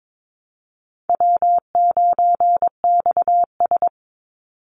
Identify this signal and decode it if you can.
Morse code, heard as a keyed tone, about 22 wpm, 700 Hz. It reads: W9XS
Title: Call Signs Repeated 2x - 22wpm